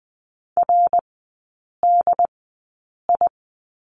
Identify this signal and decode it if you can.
Morse code, heard as a keyed tone, about 20 wpm, spaced as for 8 wpm, 700 Hz. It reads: RDI
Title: 067 Lesson 12e - Letter d - Three Characters 20wpm